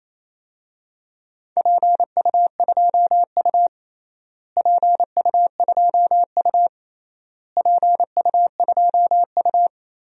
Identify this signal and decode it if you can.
Morse code, heard as a keyed tone, about 28 wpm, 700 Hz. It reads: PU2U PU2U PU2U
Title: DXCC-100 Call Signs - 4 Characters - Repeated 3x - 28wpm